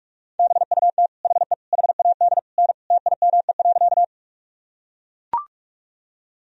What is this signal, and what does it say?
Morse code, heard as a keyed tone, about 44 wpm, 700 Hz. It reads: BUT HE HAD N TIME.
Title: Sentences from Top 100 Words and Abbreviations 45wpm